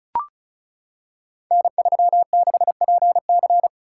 Morse code, heard as a keyed tone, about 36 wpm, 690 Hz. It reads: N36PC
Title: Code Groups - 5 Letter:Numbers - 35wpm